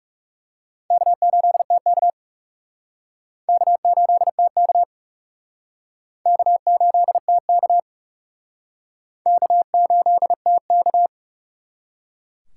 Morse code, heard as a keyed tone, about 36 wpm, 700 Hz. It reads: K8TK K8TK K8TK K8TK
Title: Speed Racing - USA Call Signs 30wpm